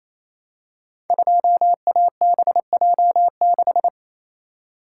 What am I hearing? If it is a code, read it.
Morse code, heard as a keyed tone, about 28 wpm, 700 Hz. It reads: 2ABJ6
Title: Code Groups - 5 Letter:Numbers - 28wpm